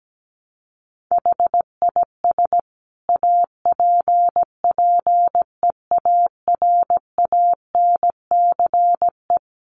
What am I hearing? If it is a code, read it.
Morse code, heard as a keyed tone, about 17 wpm, 700 Hz. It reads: HIS APPEARANCE
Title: Sentences from Top 1100 Words 17wpm